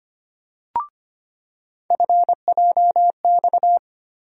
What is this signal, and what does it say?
Morse code, heard as a keyed tone, about 25 wpm, 700 Hz. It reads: FJX
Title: USA Call Sign Postfixes - 3 Letters 25wpm